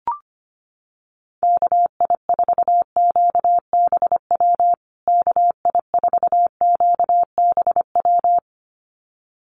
Morse code, heard as a keyed tone, about 25 wpm, 700 Hz. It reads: KI4QBW KI4QBW
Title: USA Call Signs Repeated 2x 25wpm